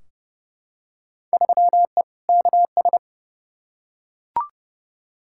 Morse code, heard as a keyed tone, about 30 wpm, 700 Hz. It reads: 3E KS
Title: Field Day Exchanges Encore - 30wpm